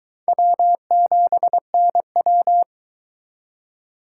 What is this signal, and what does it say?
Morse code, heard as a keyed tone, about 24 wpm, 700 Hz. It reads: W7NW